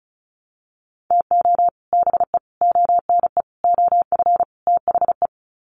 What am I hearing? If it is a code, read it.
Morse code, heard as a keyed tone, about 34 wpm, 700 Hz. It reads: TO BE ONE OF THE
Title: Sets of 5 Words 35wpm